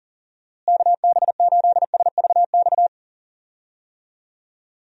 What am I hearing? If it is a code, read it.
Morse code, heard as a keyed tone, about 40 wpm, 700 Hz. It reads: KB8SVX
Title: USA Call Signs - 6 Characters - Repeated 2x - 40wpm